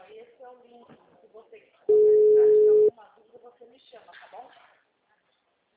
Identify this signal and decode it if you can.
Ringback tone (Europe, 425 Hz), with speech.